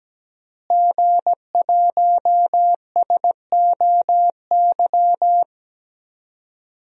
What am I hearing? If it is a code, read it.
Morse code, heard as a keyed tone, about 17 wpm, 700 Hz. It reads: G1SOY
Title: DXCC-100 Call Signs - 5 Characters - Repeated 2x - 17wpm